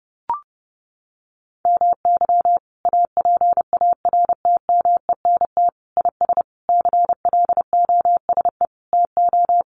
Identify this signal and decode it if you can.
Morse code, heard as a keyed tone, about 30 wpm, 700 Hz. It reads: MY APARTMENT IS CLOSE TO
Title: Sentences from Top 900 Words 30wpm